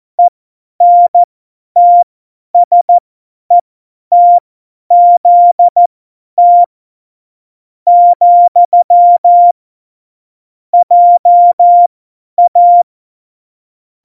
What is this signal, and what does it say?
Morse code, heard as a keyed tone, about 14 wpm, spaced as for 9 wpm, 700 Hz. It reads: ENTSETZT, JA
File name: Morsekurs / Witze / Witz 22. September 2020, 50 F70